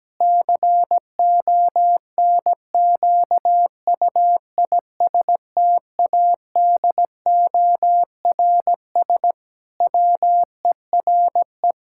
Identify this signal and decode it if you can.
Morse code, heard as a keyed tone, about 17 wpm, 700 Hz. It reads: CONQUISTADORS WERE
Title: Pangram Challenge v2 - 17wpm